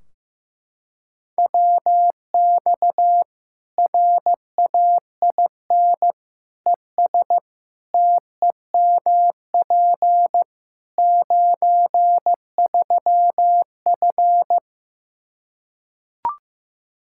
Morse code, heard as a keyed tone, about 15 wpm, 700 Hz. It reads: WX RAIN ES TEMP 93F
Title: Practice QSO - Exchange 2 - Info - 15wpm